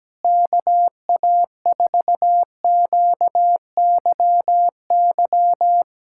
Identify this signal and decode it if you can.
Morse code, heard as a keyed tone, about 17 wpm, 700 Hz. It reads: KA4QYY
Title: USA Callsign DE Callsign 17wpm